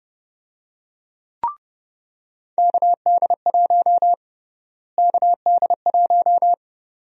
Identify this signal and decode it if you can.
Morse code, heard as a keyed tone, about 29 wpm, 700 Hz. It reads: KD1 KD1
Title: USA Call Sign Prefixes - Repeated 2x - 30wpm